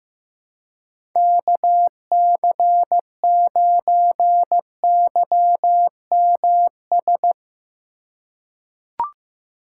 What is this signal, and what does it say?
Morse code, heard as a keyed tone, about 15 wpm, 700 Hz. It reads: KC9YMS
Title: USA Call Signs - 6 Characters - 2x Character Spacing - 15wpm